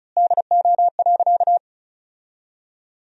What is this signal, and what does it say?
Morse code, heard as a keyed tone, about 36 wpm, 700 Hz. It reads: DO.
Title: 2x Word-Spacing - Sentences from Top 100 Words - 35wpm